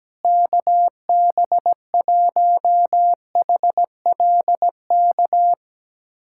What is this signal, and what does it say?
Morse code, heard as a keyed tone, about 17 wpm, 700 Hz. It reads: KB1HLK
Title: USA Callsign DE Callsign - 6 Characters - 17wpm